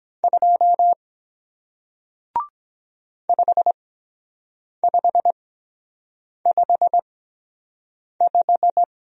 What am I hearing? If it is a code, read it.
Morse code, heard as a keyed tone, about 23 wpm, spaced as for 7 wpm, 700 Hz. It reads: 2 5555